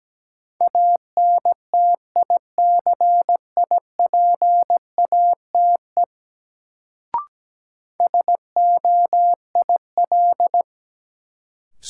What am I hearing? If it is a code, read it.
Morse code, heard as a keyed tone, about 17 wpm, 700 Hz. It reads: ANTICIPATE SOIL
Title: Sentences from Top 1500 Words - New Word Drill Alt - 17wpm